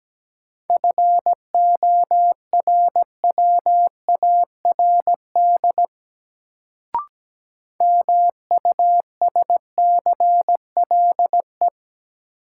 Morse code, heard as a keyed tone, about 17 wpm, 700 Hz. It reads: FORWARD MUSCLE
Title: Sentences from Top 1300 Words - New Word Drill - 17wpm